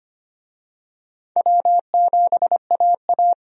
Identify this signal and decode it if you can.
Morse code, heard as a keyed tone, about 25 wpm, 700 Hz. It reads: W7AA
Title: USA Call Signs - 4 Characters - 25wpm